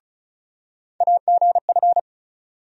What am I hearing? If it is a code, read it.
Morse code, heard as a keyed tone, about 36 wpm, 700 Hz. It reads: AGF
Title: USA Call Sign Postfixes - 3 Letters 35wpm